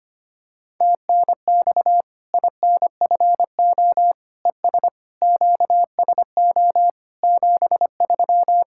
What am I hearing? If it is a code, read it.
Morse code, heard as a keyed tone, about 25 wpm, 700 Hz. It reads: TNX INFO ES QSO 73
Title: Practice QSO - Exchange 4 - Ending - 25wpm